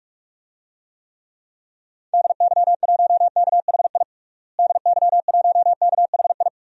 Morse code, heard as a keyed tone, about 46 wpm, 700 Hz. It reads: DY1KHI DY1KHI
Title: DXCC-100 Call Signs - 6 Characters - Repeated 2x - 45wpm